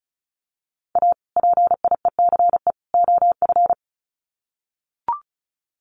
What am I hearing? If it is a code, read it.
Morse code, heard as a keyed tone, about 34 wpm, 700 Hz. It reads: A PIECE OF
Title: Sets of 3 Words 35wpm